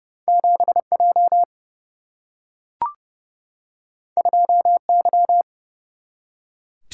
Morse code, heard as a keyed tone, about 30 wpm, 700 Hz. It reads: 7J 2Y
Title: Single-Letter Single-Number 30wpm